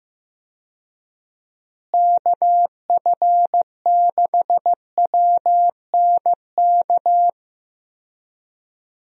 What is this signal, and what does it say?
Morse code, heard as a keyed tone, about 16 wpm, 700 Hz. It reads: KF6WNK
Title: USA Call Signs - 6 Characters - Repeated 3x - 15wpm